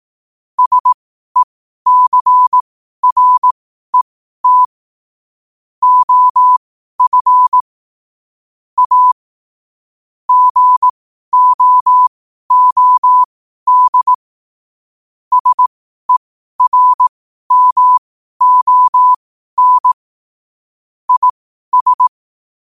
Morse code, heard as a keyed tone, about 18 wpm, spaced as for 12 wpm, 1000 Hz. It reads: SECRET OF A GOOD SERMON IS
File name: qotdfiles / 12wpm